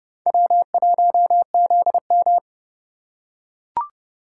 Morse code, heard as a keyed tone, about 28 wpm, 700 Hz. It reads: W1ZM